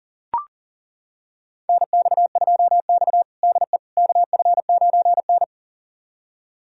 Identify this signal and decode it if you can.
Morse code, heard as a keyed tone, about 40 wpm, 700 Hz. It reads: NX2X DE KF9N
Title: USA Callsign DE Callsign - 4 Characters - 40wpm